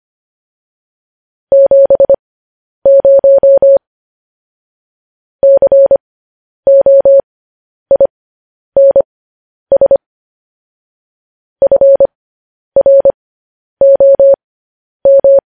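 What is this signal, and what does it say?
Morse code, heard as a keyed tone, about 25 wpm, spaced as for 10 wpm, 550 Hz. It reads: 70 COINS FROM
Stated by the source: Morse code news headlines practice transmission.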